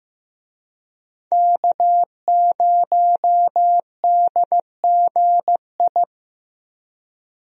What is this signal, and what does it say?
Morse code, heard as a keyed tone, about 15 wpm, 700 Hz. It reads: K0DGI
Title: USA Call Signs - 5 Characters - Repeated 2x - 15wpm